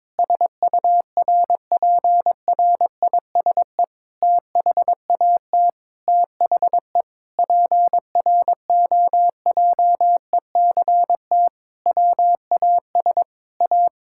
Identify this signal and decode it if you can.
Morse code, heard as a keyed tone, about 22 wpm, 700 Hz. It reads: SURPRISE THAT THE PROJECT WAS A
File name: practice / 22wpm / Sentences with 8-Word Phrases - 22wpm